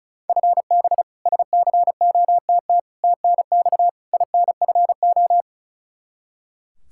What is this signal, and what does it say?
Morse code, heard as a keyed tone, about 34 wpm, 700 Hz. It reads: FB SCOTT TNX INFO
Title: K1USN SST QSO Practice 35wpm